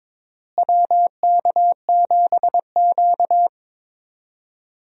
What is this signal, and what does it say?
Morse code, heard as a keyed tone, about 21 wpm, 700 Hz. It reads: WK7Q
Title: USA Call Signs - 4 Characters - Repeated 3x - 22wpm